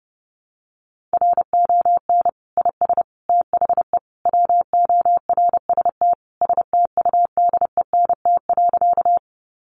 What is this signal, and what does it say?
Morse code, heard as a keyed tone, about 30 wpm, 700 Hz. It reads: RON IS THE WORST STUDENT.
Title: Additional Easy Phrases 30wpm